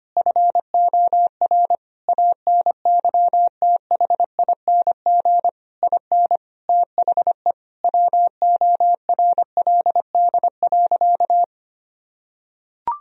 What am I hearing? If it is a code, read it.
Morse code, heard as a keyed tone, about 25 wpm, 700 Hz. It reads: FOR ANYTHING IN THE WORLD.
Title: Sentences from Top 1000 Words - Review Part 3 - 25wpm